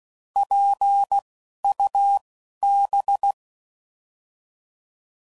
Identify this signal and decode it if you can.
Morse code, heard as a keyed tone, about 16 wpm, spaced as for 11 wpm, 800 Hz. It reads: PUB
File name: Prefixes / By First Letter / Ps